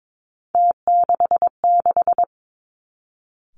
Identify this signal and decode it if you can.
Morse code, heard as a keyed tone, about 22 wpm, 700 Hz. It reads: T66
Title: Three Challenging Letter-Number Combinations 22wpm